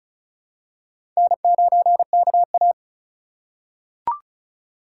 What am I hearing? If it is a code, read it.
Morse code, heard as a keyed tone, about 36 wpm, 700 Hz. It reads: N9KA
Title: USA Call Signs - 4 Characters - 3x Character Spacing - 35wpm